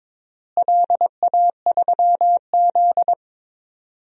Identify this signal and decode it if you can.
Morse code, heard as a keyed tone, about 22 wpm, 700 Hz. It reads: LA3Z
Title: DXCC-100 Call Signs - 4 Characters - Repeated 3x - 22wpm